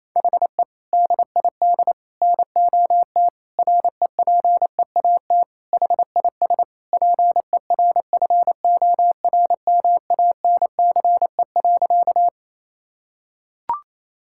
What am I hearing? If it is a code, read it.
Morse code, heard as a keyed tone, about 28 wpm, 700 Hz. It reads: HE DID NOT REPEAT HIS PERFORMANCE.
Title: Sentences from Top 1600 Words 28wpm